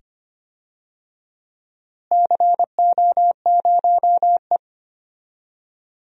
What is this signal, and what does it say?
Morse code, heard as a keyed tone, about 25 wpm, 700 Hz. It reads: CO0E
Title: DXCC-100 Call Signs - 4 Characters - Repeated 2x - 25wpm